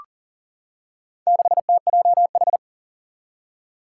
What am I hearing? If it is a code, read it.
Morse code, heard as a keyed tone, about 39 wpm, 700 Hz. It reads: 6TJH
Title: Four Challenging Letter-Number Combinations 40wpm